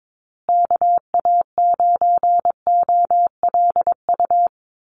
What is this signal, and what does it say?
Morse code, heard as a keyed tone, about 22 wpm, 700 Hz. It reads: KA9OLU
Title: USA Call Signs Repeated 2x 22wpm